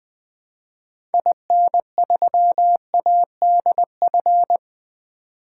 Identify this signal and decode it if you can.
Morse code, heard as a keyed tone, about 20 wpm, 700 Hz. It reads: IN3ADF
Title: 2023 SOTA Activator Call Signs - 20wpm